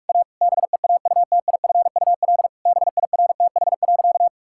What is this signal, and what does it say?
Morse code, heard as a keyed tone, about 46 wpm, 700 Hz. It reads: A BEAUTIFUL BIRTH.
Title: Sentences from Top 1000 Words - Review Part 5 - 45wpm